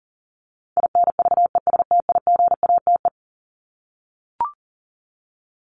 Morse code, heard as a keyed tone, about 40 wpm, 700 Hz. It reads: INVESTIGATE
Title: Top 1751-2000 Words 40wpm